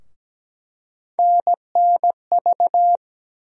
Morse code, heard as a keyed tone, about 17 wpm, 700 Hz. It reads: NNV
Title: USA Call Sign Postfixes - 3 Letters 17wpm